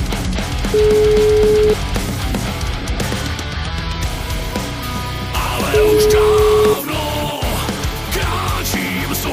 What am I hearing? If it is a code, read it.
Ringback tone (Europe, 425 Hz), with noise.